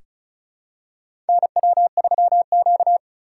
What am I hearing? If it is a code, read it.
Morse code, heard as a keyed tone, about 36 wpm, 700 Hz. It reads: NW3Q
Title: USA Call Signs - 4 Characters - 3x Character Spacing - 35wpm